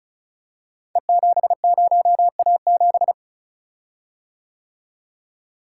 Morse code, heard as a keyed tone, about 36 wpm, 700 Hz. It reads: E70A7
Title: Code Groups - 5 Letter:Numbers - 35wpm